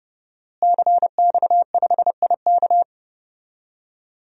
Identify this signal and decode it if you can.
Morse code, heard as a keyed tone, about 31 wpm, 700 Hz. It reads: CX5IK
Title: DXCC-100 Call Signs - 5 Characters - Repeated 2x - 30wpm